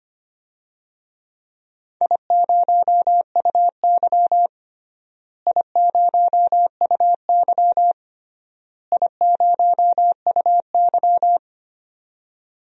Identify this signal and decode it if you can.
Morse code, heard as a keyed tone, about 25 wpm, 700 Hz. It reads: I0UY I0UY I0UY
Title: DXCC-100 Call Signs - 4 Characters - Repeated 3x - 25wpm